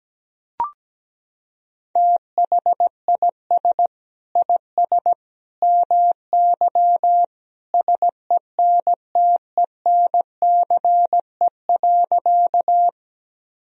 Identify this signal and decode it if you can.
Morse code, heard as a keyed tone, about 17 wpm, 700 Hz. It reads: THIS IS MY SENTENCE.
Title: Sentences from Top 1000 Words - Review Part 1 - 17wpm